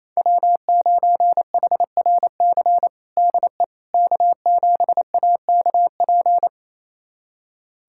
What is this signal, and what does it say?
Morse code, heard as a keyed tone, about 27 wpm, 700 Hz. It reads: W9HRC DE K7AKP
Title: USA Callsign DE Callsign - 5 Characters - 28wpm